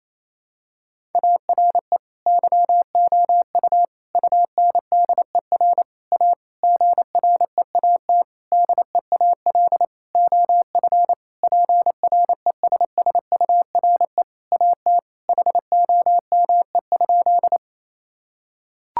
Morse code, heard as a keyed tone, about 28 wpm, 700 Hz. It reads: ARE YOU UNDER A GREAT DEAL OF PRESSURE AT HOME?
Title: Sentences from Top 1000 Words - Review Part 6 - 28wpm